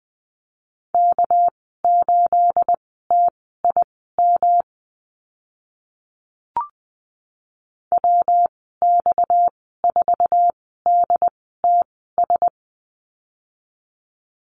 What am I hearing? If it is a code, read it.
Morse code, heard as a keyed tone, about 20 wpm, spaced as for 14 wpm, 700 Hz. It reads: K8TIM WX4DTS
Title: 231 Lesson 37g - Letter x - Call Signs 20wpm